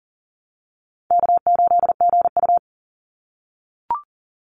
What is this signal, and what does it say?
Morse code, heard as a keyed tone, about 40 wpm, 700 Hz. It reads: K8GU
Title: Easy Call Signs - 40wpm